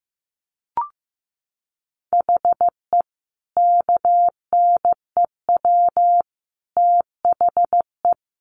Morse code, heard as a keyed tone, about 15 wpm, 700 Hz. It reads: HE KNEW THE
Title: Sentences from Top 1400 Words 15wpm